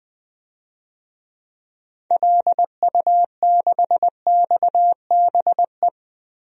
Morse code, heard as a keyed tone, about 20 wpm, 700 Hz. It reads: LU6XBE
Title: DXCC-100 Call Signs - 6 Characters - Repeated 2x - 20wpm